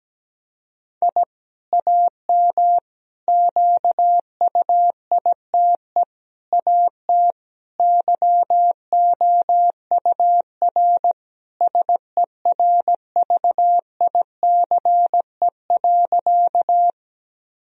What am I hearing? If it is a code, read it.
Morse code, heard as a keyed tone, about 17 wpm, 700 Hz. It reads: I AM QUITE AT YOUR SERVICE.
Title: Sentences from Top 1000 Words - Review Part 4 - 17wpm